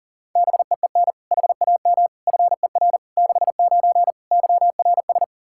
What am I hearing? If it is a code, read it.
Morse code, heard as a keyed tone, about 40 wpm, 700 Hz. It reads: BEEN HAM FER 69 YRS
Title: Practice QSO - Exchange 3 - More Info - 40wpm